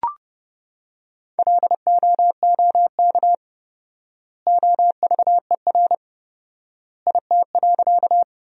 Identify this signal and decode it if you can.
Morse code, heard as a keyed tone, about 30 wpm, 700 Hz. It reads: LOOK OVER IT.
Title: 4x Word-Spacing - Sentences from Top 100 Words - 30wpm